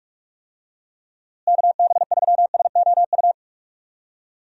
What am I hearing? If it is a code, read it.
Morse code, heard as a keyed tone, about 46 wpm, 700 Hz. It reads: KB3SOU
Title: USA Call Signs - 6 Characters - Repeated 3x - 45wpm